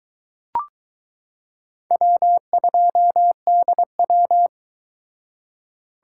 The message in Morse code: W2DW

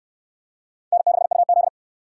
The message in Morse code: N6UZ

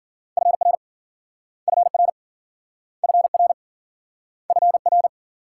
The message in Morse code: FR FR FR FR

